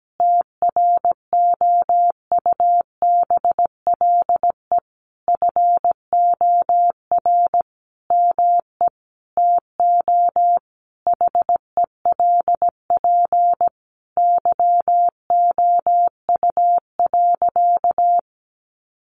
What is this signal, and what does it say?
Morse code, heard as a keyed tone, about 17 wpm, 700 Hz. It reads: TROUBLE FOR ME TO HELP YOU.
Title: Sentences from Top 1200 Words 17wpm